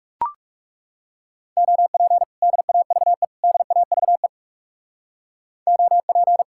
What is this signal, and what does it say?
Morse code, heard as a keyed tone, about 38 wpm, 700 Hz. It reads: OP DAVE DAVE OP
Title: Speed-Racing Common QSO Phrases Encore - 30wpm